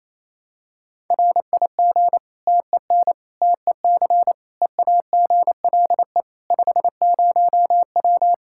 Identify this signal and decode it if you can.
Morse code, heard as a keyed tone, about 28 wpm, 700 Hz. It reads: RIG TEN TEC EAGLE 50W
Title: Practice QSO - Exchange 2 - Info - 28wpm